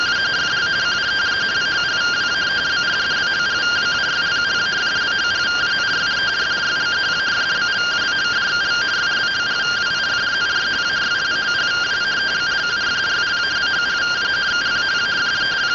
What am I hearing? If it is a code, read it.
Two-tone teleprinter signal, about 170 Hz shift.
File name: findings / 81 axial iodine 262.gen-20 rtty rx